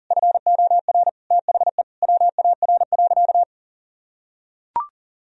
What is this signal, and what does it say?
Morse code, heard as a keyed tone, about 40 wpm, 700 Hz. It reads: FOR THE WAR.
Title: Sentences from Top 1000 Words - Review Part 4 - 40wpm